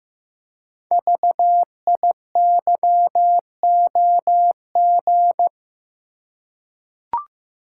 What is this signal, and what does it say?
Morse code, heard as a keyed tone, about 15 wpm, 700 Hz. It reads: VIYOG
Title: Code Groups - 5 Letters - 15wpm